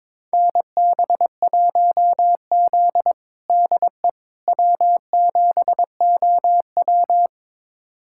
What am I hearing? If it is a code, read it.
Morse code, heard as a keyed tone, about 22 wpm, 700 Hz. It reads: NB1Z DE W7OW
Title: USA Callsign DE Callsign - 4 Characters - 22wpm